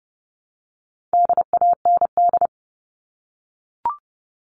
Morse code, heard as a keyed tone, about 30 wpm, 700 Hz. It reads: DAND
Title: All Cut Numbers - 4-Digit 30wpm